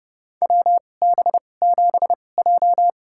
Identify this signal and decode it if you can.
Morse code, heard as a keyed tone, about 31 wpm, spaced as for 22 wpm, 700 Hz. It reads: WB7J
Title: USA Call Signs - 4 Characters - 2x Character Spacing - 30wpm